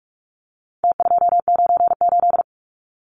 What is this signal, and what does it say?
Morse code, heard as a keyed tone, about 44 wpm, 700 Hz. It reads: T298
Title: Numbers - Contest Style - 4-Digit 45wpm